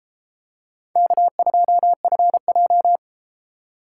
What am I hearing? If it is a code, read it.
Morse code, heard as a keyed tone, about 32 wpm, 700 Hz. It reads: K2FJ